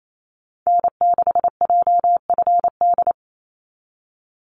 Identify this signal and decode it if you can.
Morse code, heard as a keyed tone, about 28 wpm, 700 Hz. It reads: N6JFD
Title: NA SOTA Activator Callsigns 28wpm